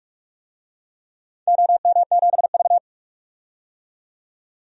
Morse code, heard as a keyed tone, about 41 wpm, 700 Hz. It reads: OM7V
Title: DXCC-100 Call Signs - 4 Characters - 45wpm